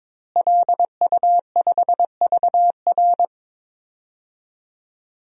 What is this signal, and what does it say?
Morse code, heard as a keyed tone, about 21 wpm, 700 Hz. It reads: LU5VR
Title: DXCC-100 Call Signs - 5 Characters - Repeated 3x - 22wpm